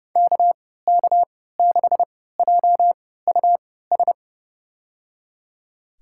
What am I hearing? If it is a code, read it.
Morse code, heard as a keyed tone, about 30 wpm, spaced as for 17 wpm, 700 Hz. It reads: KK6JUS
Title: USA Call Signs - 3x Character Spacing - 30wpm